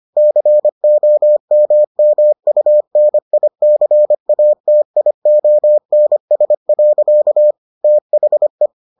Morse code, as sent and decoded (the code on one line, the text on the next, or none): -.-. --- -- -- ..- -. .. -.-. .- - .. --- -. ... .-.-.- / - .... .
COMMUNICATIONS. THE